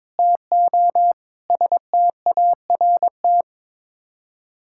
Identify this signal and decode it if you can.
Morse code, heard as a keyed tone, about 22 wpm, 700 Hz. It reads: TO START
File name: practice / 22wpm / Sets of 6 Words - 22wpm